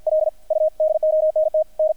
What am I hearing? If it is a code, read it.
Morse code, heard as a keyed tone, about 50 wpm, 610 Hz, with radio noise. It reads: L U DONT N
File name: archived / CRSnet